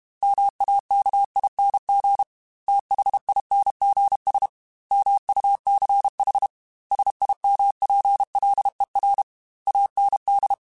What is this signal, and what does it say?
Morse code, heard as a keyed tone, about 32 wpm, 800 Hz. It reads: MAKING THINGS MUCH SIMPLER AND
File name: Technical Subjects / Technical Subjects-mp3s / AND Gate @30